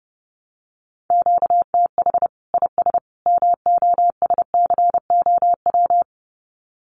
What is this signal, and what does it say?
Morse code, heard as a keyed tone, about 30 wpm, 700 Hz. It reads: QTH IS MOSCOW
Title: Common QSO Phrases 30wpm